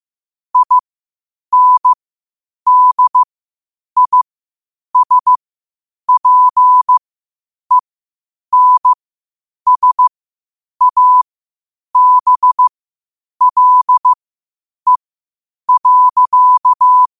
Morse code, heard as a keyed tone, about 15 wpm, spaced as for 8 wpm, 1000 Hz. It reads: INDISPENSABLE.
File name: qotdfiles / 8wpm